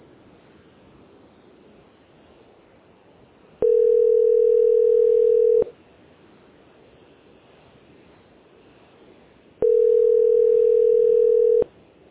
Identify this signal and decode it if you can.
Ringback tone (North America), 440 + 480 Hz.
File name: c1-5e86-4ccd-9879